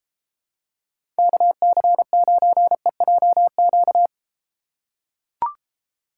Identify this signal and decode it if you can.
Morse code, heard as a keyed tone, about 32 wpm, 700 Hz. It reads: KC9EJQ